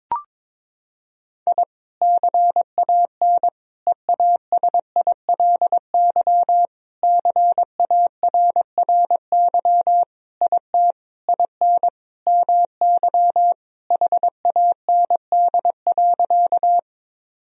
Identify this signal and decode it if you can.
Morse code, heard as a keyed tone, about 22 wpm, 700 Hz. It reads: I CAN EASILY CARRY IT IN MY HAND.
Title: Sentences from Top 1000 Words - Review Part 1 - 22wpm